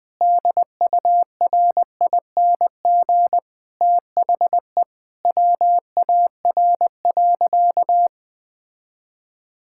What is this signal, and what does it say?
Morse code, heard as a keyed tone, about 20 wpm, 700 Hz. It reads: DURING THE WAR.
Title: Sentences from Top 1000 Words - Review Part 6 - 20wpm